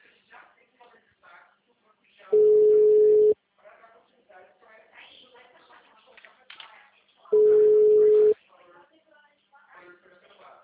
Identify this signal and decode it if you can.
Ringback tone (Europe, 425 Hz), with speech.